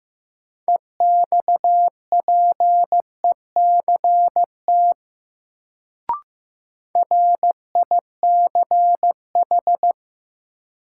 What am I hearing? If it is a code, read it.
Morse code, heard as a keyed tone, about 15 wpm, 700 Hz. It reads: EXPECT RICH
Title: Sentences from Top 1100 Words - New Word Drill - 15wpm